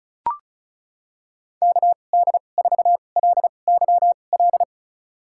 Morse code, heard as a keyed tone, about 36 wpm, spaced as for 25 wpm, 700 Hz. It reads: KD4LYL
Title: USA Call Signs - 2x Character Spacing - 35wpm